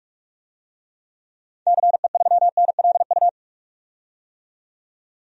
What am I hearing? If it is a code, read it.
Morse code, heard as a keyed tone, about 45 wpm, 700 Hz. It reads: CE3NLU
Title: DXCC-100 Call Signs - 6 Characters - 45wpm